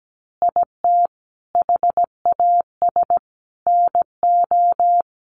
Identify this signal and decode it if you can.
Morse code, heard as a keyed tone, about 17 wpm, 700 Hz. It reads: IT HAS NO
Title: Easy Phrases 17wpm